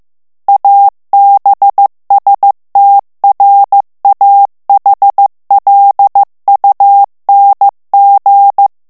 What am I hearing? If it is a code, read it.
Morse code, heard as a keyed tone, about 15 wpm, 800 Hz. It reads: ABSTRAHLUNG
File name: morse mp3 / ALLEMAND texte clair 1 vitesse 14 mots minute